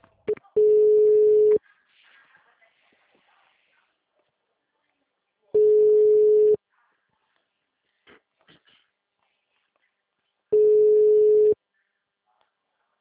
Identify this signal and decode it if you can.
Ringback tone (Europe, 425 Hz), with speech.